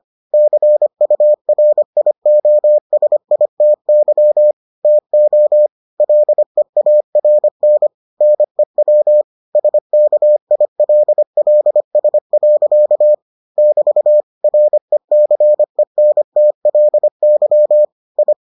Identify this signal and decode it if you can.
Morse code, heard as a keyed tone, about 25 wpm, 600 Hz. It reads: CURIOSITY TO LEARN NEW SKILLS. <BT> RECENTLY I